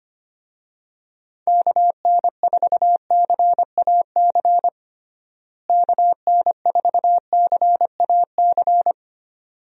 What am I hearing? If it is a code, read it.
Morse code, heard as a keyed tone, about 25 wpm, 700 Hz. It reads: KN4CAC KN4CAC
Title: USA Call Signs - 6 Characters - Repeated 2x - 25wpm